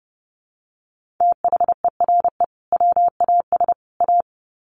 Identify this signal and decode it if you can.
Morse code, heard as a keyed tone, about 30 wpm, 700 Hz. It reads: THERE WAS A
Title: Sets of 3 Words 30wpm